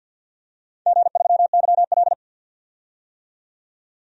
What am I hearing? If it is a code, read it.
Morse code, heard as a keyed tone, about 50 wpm, 700 Hz. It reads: G3YL
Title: DXCC-100 Call Signs - 4 Characters - 50wpm